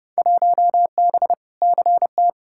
Morse code, heard as a keyed tone, about 30 wpm, 700 Hz. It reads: 1B CT
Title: Field Day Exchanges Encore - 30wpm